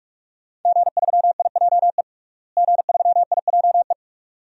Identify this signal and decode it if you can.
Morse code, heard as a keyed tone, about 46 wpm, 700 Hz. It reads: G3IJE G3IJE
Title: DXCC-100 Call Signs - 5 Characters - Repeated 2x - 45wpm